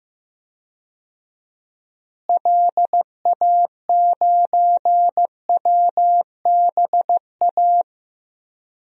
Morse code, heard as a keyed tone, about 14 wpm, 700 Hz. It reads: LA9WBA
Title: DXCC-100 Call Signs - 6 Characters - Repeated 3x - 15wpm